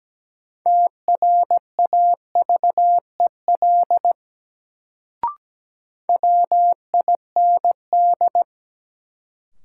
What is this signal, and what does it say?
Morse code, heard as a keyed tone, about 17 wpm, 700 Hz. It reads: TRAVEL WIND
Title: Sentences from Top 400 Words - New Word Drill - 17wpm